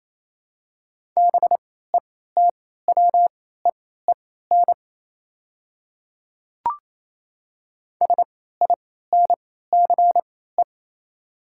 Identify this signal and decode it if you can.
Morse code, heard as a keyed tone, about 28 wpm, spaced as for 16 wpm, 700 Hz. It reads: BETWEEN SINCE
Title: 3x Character Spacing - Top 300 Words - 28wpm